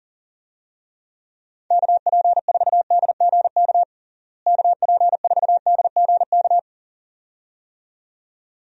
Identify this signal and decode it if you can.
Morse code, heard as a keyed tone, about 37 wpm, 700 Hz. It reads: KP4DGK KP4DGK
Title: USA Call Signs - 6 Characters - Repeated 2x - 40wpm